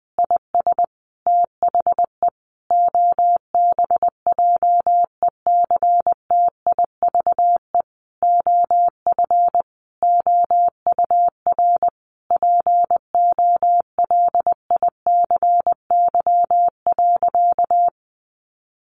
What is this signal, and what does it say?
Morse code, heard as a keyed tone, about 20 wpm, 700 Hz. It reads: IS THE OBJECTIVE OF OUR POLICY.
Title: Sentences from Top 1200 Words 20wpm